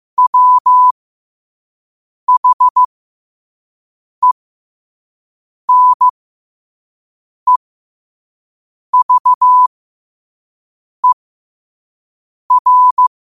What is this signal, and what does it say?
Morse code, heard as a keyed tone, about 15 wpm, spaced as for 5 wpm, 1000 Hz. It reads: WHENEVER
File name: qotdfiles / 5wpm